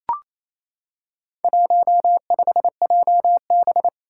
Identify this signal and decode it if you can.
Morse code, heard as a keyed tone, about 28 wpm, 700 Hz. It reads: 15JB
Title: Four Challenging Letter-Number Combinations 28wpm